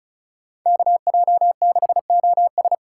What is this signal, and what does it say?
Morse code, heard as a keyed tone, about 34 wpm, 700 Hz. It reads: KJ6OS
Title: USA Call Signs - 2x Character Spacing - 35wpm